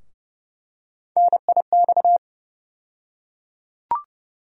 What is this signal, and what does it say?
Morse code, heard as a keyed tone, about 32 wpm, 700 Hz. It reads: NIX
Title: USA Call Sign Postfixes - 3 Letters 30wpm